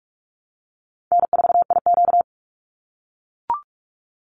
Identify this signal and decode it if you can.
Morse code, heard as a keyed tone, about 44 wpm, 700 Hz. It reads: N4IQ
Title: Easy Call Signs - 45wpm